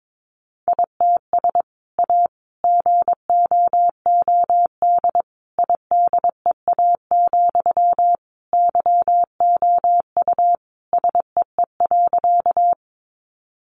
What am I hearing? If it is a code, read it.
Morse code, heard as a keyed tone, about 22 wpm, 700 Hz. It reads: ITS A GOOD IDEA, YOU SEE.
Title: Sentences from Top 600 Words 22wpm